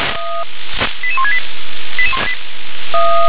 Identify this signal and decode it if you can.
Five-tone selective call (ZVEI), heard as quick runs of five tones, tones of about 70 ms. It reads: 88168